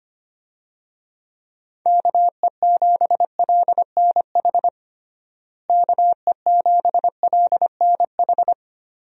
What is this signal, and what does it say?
Morse code, heard as a keyed tone, about 25 wpm, 700 Hz. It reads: KE7LNH KE7LNH
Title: USA Call Signs - 6 Characters - Repeated 3x - 25wpm